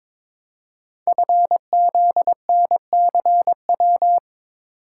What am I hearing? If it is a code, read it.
Morse code, heard as a keyed tone, about 22 wpm, 700 Hz. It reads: FZNCW
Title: Code Groups - 5 Letters - 22wpm